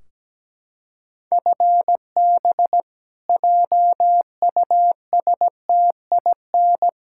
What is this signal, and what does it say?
Morse code, heard as a keyed tone, about 17 wpm, 700 Hz. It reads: FB JUSTIN
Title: Practice QSO - Exchange 3 - More Info - 17wpm